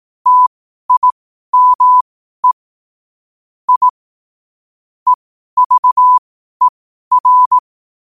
Morse code, heard as a keyed tone, about 18 wpm, spaced as for 12 wpm, 1000 Hz. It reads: TIME I EVER
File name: qotdfiles / 12wpm